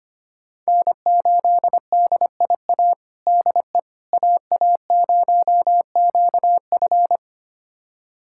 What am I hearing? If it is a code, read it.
Morse code, heard as a keyed tone, about 25 wpm, 700 Hz. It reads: N8DIA DE AA0QF
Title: USA Callsign DE Callsign - 5 Characters - 25wpm